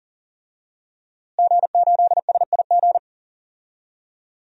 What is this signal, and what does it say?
Morse code, heard as a keyed tone, about 38 wpm, 700 Hz. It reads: G8SIG
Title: DXCC-100 Call Signs - 5 Characters - Repeated 3x - 40wpm